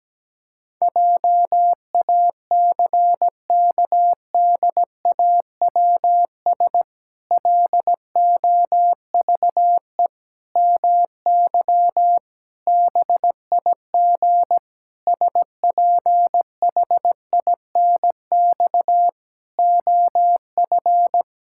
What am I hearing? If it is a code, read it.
Morse code, heard as a keyed tone, about 17 wpm, 700 Hz. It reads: JACKDAWS LOVE MY BIG SPHINX OF
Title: Pangram Challenge v2 - 17wpm